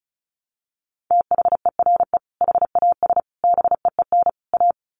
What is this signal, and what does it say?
Morse code, heard as a keyed tone, about 34 wpm, 700 Hz. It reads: THERE HAS BEEN A
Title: Sets of 4 Words 35wpm